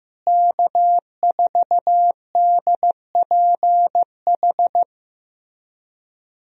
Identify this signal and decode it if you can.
Morse code, heard as a keyed tone, about 15 wpm, 700 Hz. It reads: K4DPH
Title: USA Call Signs - 5 Characters - Repeated 3x - 15wpm